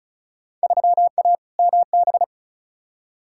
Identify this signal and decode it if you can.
Morse code, heard as a keyed tone, about 36 wpm, 690 Hz. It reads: 3A MB
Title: Field Day Exchanges Encore - 35wpm